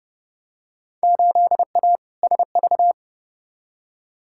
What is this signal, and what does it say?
Morse code, heard as a keyed tone, about 30 wpm, 700 Hz. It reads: 8A SV
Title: Field Day Exchanges Encore - 30wpm